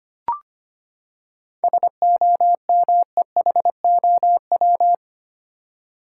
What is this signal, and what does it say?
Morse code, heard as a keyed tone, about 25 wpm, 700 Hz. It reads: SOMEHOW
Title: Seven Letter Words - 25wpm